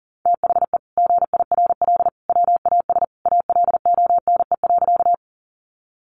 Morse code, heard as a keyed tone, about 40 wpm, 700 Hz. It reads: THE GIRL WAS ALONE.
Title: Sentences from Top 1000 Words 40wpm